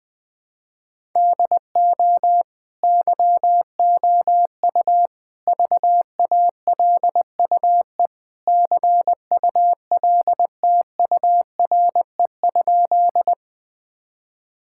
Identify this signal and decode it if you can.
Morse code, heard as a keyed tone, about 20 wpm, 700 Hz. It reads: DO YOU VALUE CULTURE?
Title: Sentences from Top 1000 Words - Review Part 6 - 20wpm